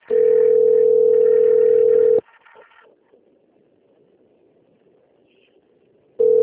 Ringback tone (North America), 440 + 480 Hz.